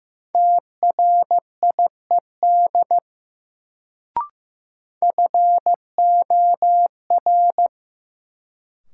Morse code, heard as a keyed tone, about 15 wpm, 700 Hz. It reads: TRIED FOR
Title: Sets of 2 Words - New Word Drill - 15wpm